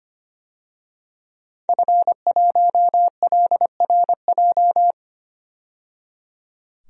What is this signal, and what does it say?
Morse code, heard as a keyed tone, about 25 wpm, 700 Hz. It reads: F1LRJ
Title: DXCC-100 Call Signs - 5 Characters - 25wpm